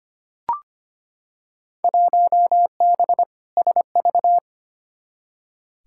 Morse code, heard as a keyed tone, about 25 wpm, 700 Hz. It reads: 1B SV
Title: Field Day Exchanges - 25wpm